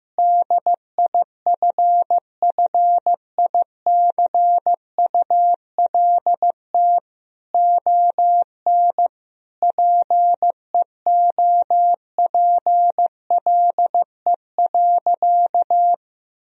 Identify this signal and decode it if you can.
Morse code, heard as a keyed tone, about 16 wpm, 700 Hz. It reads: DIFFICULT ON PEOPLE.
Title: Sentences from Top 1000 Words - Review Part 3 - 15wpm